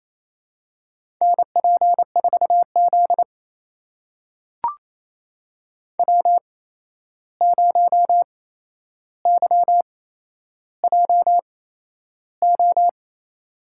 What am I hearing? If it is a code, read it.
Morse code, heard as a keyed tone, about 28 wpm, spaced as for 11 wpm, 700 Hz. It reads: NP4Z W0YJO
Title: USA Call Signs - 8x Character Spacing - 28wpm